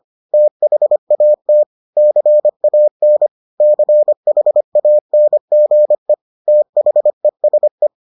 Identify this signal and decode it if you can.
Morse code, heard as a keyed tone, about 25 wpm, 600 Hz. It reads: THAT CAN CHANGE THESE